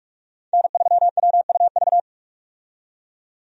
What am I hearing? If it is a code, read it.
Morse code, heard as a keyed tone, about 41 wpm, 700 Hz. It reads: N3WUV
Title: USA Call Signs - 5 Characters - 45wpm